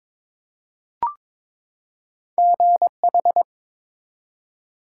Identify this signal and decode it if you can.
Morse code, heard as a keyed tone, about 22 wpm, 700 Hz. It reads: GH
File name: ARRL Sections v2 - 22wpm